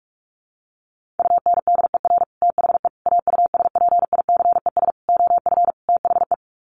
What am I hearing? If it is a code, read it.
Morse code, heard as a keyed tone, about 44 wpm, 700 Hz. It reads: UNDER THE AUSPICES OF THE
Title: Sets of 5 Words Encore2 45wpm